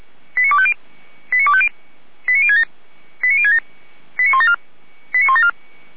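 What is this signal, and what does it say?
Five-tone selective call (ZVEI), heard as quick runs of five tones, tones of about 70 ms. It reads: 89250 89250 89067 89067 89174
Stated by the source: ZVEI five-tone selective call 89240 received by SDR, from the fire brigade Feuerwehr Orlamünde audio archive.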